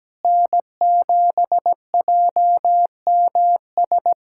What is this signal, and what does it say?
Morse code, heard as a keyed tone, about 17 wpm, 700 Hz. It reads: N7JMS
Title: USA Callsign DE Callsign - 5 Characters - 17wpm